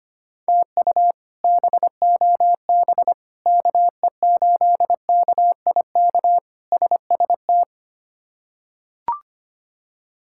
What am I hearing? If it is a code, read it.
Morse code, heard as a keyed tone, about 25 wpm, 700 Hz. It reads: TU BOB KE8KIK SST
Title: K1USN SST QSO Practice - 25wpm